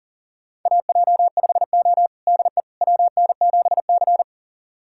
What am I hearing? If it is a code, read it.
Morse code, heard as a keyed tone, about 40 wpm, 700 Hz. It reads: AJ5O DE WN7C
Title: USA Callsign DE Callsign - 4 Characters - 40wpm